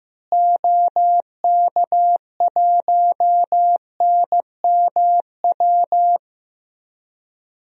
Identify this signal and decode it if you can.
Morse code, heard as a keyed tone, about 15 wpm, 700 Hz. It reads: OK1NMW
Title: DXCC-100 Call Signs - 6 Characters - Repeated 3x - 15wpm